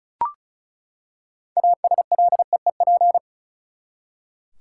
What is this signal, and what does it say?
Morse code, heard as a keyed tone, about 36 wpm, 700 Hz. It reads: ASLEEP
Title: Six Letter Words - 35wpm